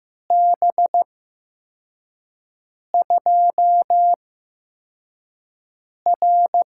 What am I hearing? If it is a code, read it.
Morse code, heard as a keyed tone, about 15 wpm, spaced as for 4 wpm, 700 Hz. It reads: B2R
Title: USA Call Signs - 8x Character Spacing - 15wpm